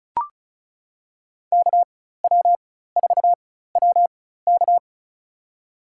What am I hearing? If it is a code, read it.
Morse code, heard as a keyed tone, about 36 wpm, spaced as for 16 wpm, 700 Hz. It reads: KW4WK
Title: USA Call Signs - 5 Characters - 4x Character Spacing - 35wpm